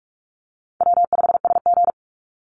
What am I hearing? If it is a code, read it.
Morse code, heard as a keyed tone, about 44 wpm, 700 Hz. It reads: W5SG
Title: Call Signs Repeated 2x - 45wpm